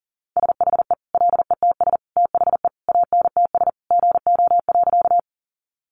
Morse code, heard as a keyed tone, about 40 wpm, 700 Hz. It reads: SHE LETS THE ANTS GO.
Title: Additional Easy Phrases 40wpm